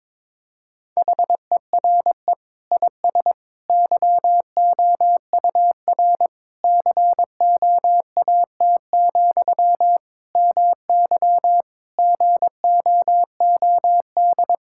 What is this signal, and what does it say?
Morse code, heard as a keyed tone, about 21 wpm, 700 Hz. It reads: HERE IS YOUR COAT, MY GOOD
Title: Sentences from Top 1600 Words 22wpm